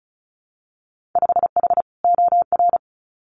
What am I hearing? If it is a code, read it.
Morse code, heard as a keyed tone, about 36 wpm, 700 Hz. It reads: 5H OR
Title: Winter Field Day Exchanges Encore 35wpm